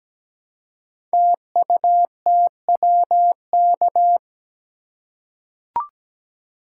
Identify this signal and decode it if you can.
Morse code, heard as a keyed tone, about 17 wpm, 700 Hz. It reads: TUTWK
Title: Code Groups - 5 Letters - 17wpm